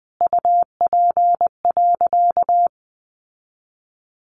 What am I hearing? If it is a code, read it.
Morse code, heard as a keyed tone, about 20 wpm, 700 Hz. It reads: UP.
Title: Sentences from Top 1000 Words 20wpm